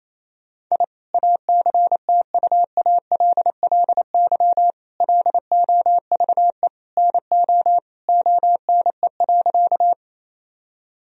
Morse code, heard as a keyed tone, about 27 wpm, 700 Hz. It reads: I ACTUALLY LOVE NO ONE.
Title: Sentences from Top 1000 Words - Review Part 4 - 28wpm